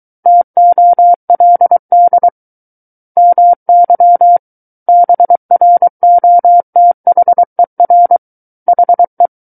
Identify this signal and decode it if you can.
Morse code, heard as a keyed tone, about 23 wpm, 700 Hz. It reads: TOLD MY BROTHER HE
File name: Book One / 14-Chapter 14 - 23 WPM